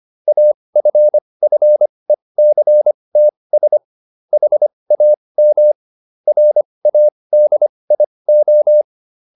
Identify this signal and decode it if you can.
Morse code, heard as a keyed tone, about 25 wpm, spaced as for 20 wpm, 600 Hz. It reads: AFFECTS HAM RADIO